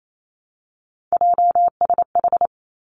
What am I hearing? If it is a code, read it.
Morse code, heard as a keyed tone, about 28 wpm, 700 Hz. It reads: JSH
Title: Three Challenging Letter-Number Combinations 28wpm